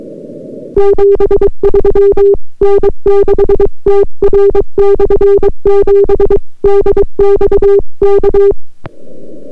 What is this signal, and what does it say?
Morse code, heard as a keyed tone, about 23 wpm, 400 Hz. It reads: 73 N6TR/7 DX K